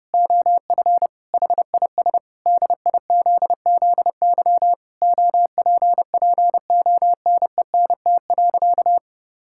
Morse code, heard as a keyed tone, about 30 wpm, 700 Hz. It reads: OF HIS DIZZY OPPONENT.
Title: Pangram Challenge v2 - 30wpm